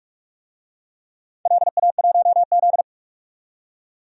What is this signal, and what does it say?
Morse code, heard as a keyed tone, about 47 wpm, 700 Hz. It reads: LA1Z
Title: DXCC-100 Call Signs - 4 Characters - 45wpm